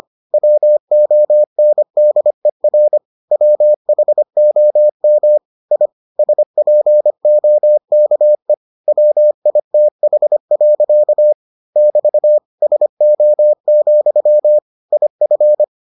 Morse code, heard as a keyed tone, about 25 wpm, 600 Hz. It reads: WONDER WHOM I SPOKE WITH. <BT> SO, IF